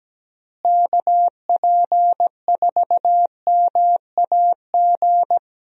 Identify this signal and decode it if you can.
Morse code, heard as a keyed tone, about 17 wpm, 700 Hz. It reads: KP4MAG
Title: USA Call Signs - 6 Characters - Repeated 3x - 17wpm